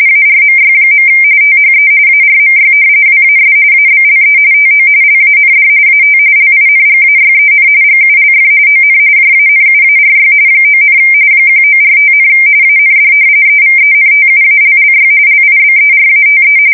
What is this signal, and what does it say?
RTTY, about 45.45 baud. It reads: ARMY. THE ARMY WAS ABLE TO LAND AT LINGAYEN GULF ON JANUARY 9, 1945, AND EVENTUALLY LIBERATE
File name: fsk170 / battleshipnj